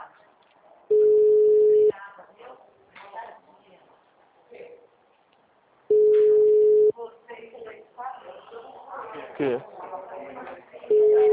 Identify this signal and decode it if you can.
Ringback tone (Europe, 425 Hz), with speech.